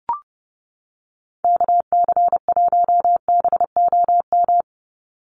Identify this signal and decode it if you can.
Morse code, heard as a keyed tone, about 29 wpm, 700 Hz. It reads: KC1BOM
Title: USA Call Signs 30wpm